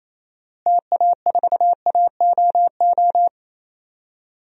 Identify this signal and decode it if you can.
Morse code, heard as a keyed tone, about 28 wpm, 700 Hz. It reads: TA4AOO
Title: DXCC-100 Call Signs - 6 Characters - Repeated 3x - 28wpm